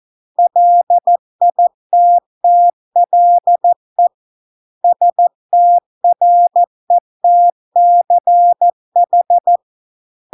Morse code, heard as a keyed tone, about 14 wpm, 700 Hz. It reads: LITTLE STRETCH